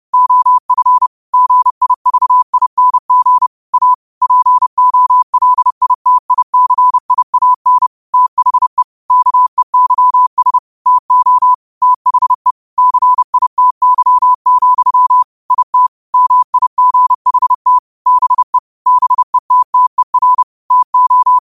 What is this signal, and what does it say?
Morse code, heard as a keyed tone, about 30 wpm, 1000 Hz. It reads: OF GIVING A POLITICIAN THE KEYS TO THE CITY, IT MIGHT BE BETTER TO